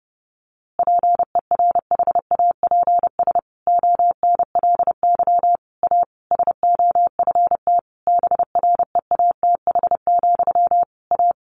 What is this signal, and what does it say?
Morse code, heard as a keyed tone, about 30 wpm, 700 Hz. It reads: PERHAPS ONLY A SOFT BREATH, A
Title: A Sound Of Thunder 30wpm